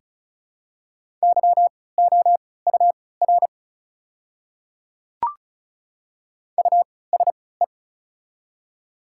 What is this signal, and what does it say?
Morse code, heard as a keyed tone, about 34 wpm, spaced as for 20 wpm, 700 Hz. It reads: YOUR USE
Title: 3x Character Spacing - Top 500 Words - 35wpm